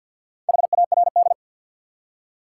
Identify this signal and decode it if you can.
Morse code, heard as a keyed tone, about 53 wpm, 700 Hz. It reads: HARD
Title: Four Letter Words - 50wpm